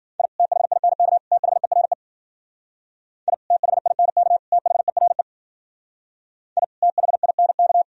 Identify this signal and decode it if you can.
Morse code, heard as a keyed tone, about 51 wpm, 700 Hz. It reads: I THINK THERE I THINK THERE I THINK